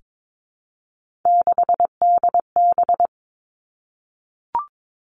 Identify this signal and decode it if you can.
Morse code, heard as a keyed tone, about 22 wpm, 700 Hz. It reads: 6DB
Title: Three Challenging Letter-Number Combinations 22wpm